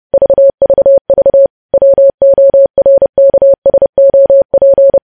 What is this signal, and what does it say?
Morse code, heard as a keyed tone, about 30 wpm, 550 Hz. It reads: VVV WORKSOP
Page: news headlines in morse code at 30 wpm